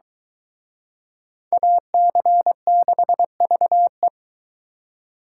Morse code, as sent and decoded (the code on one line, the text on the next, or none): .- -.-. -.... ...- .
AC6VE